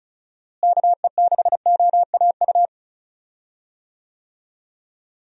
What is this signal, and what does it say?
Morse code, heard as a keyed tone, about 34 wpm, 700 Hz. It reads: KE6OAU
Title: USA Call Signs - 6 Characters - Repeated 3x - 35wpm